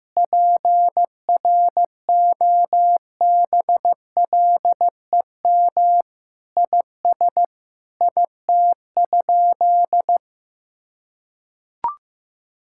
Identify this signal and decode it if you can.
Morse code, heard as a keyed tone, about 16 wpm, 700 Hz. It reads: PROBLEM IS IT?
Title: Sentences from Top 1000 Words - Review Part 6 - 15wpm